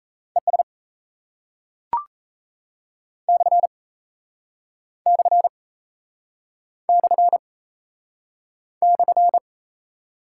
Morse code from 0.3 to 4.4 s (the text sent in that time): ES /